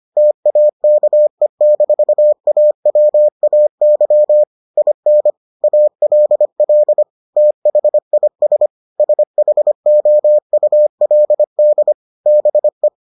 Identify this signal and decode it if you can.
Morse code, heard as a keyed tone, about 25 wpm, 600 Hz. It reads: TAKE-AWAY IN ALL THIS SHOULD BE